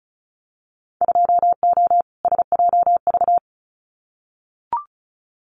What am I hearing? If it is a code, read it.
Morse code, heard as a keyed tone, about 36 wpm, 700 Hz. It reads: 2O SJV
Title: Winter Field Day Exchanges Encore 35wpm